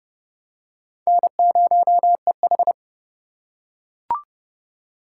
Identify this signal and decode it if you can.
Morse code, heard as a keyed tone, about 29 wpm, 700 Hz. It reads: N0EH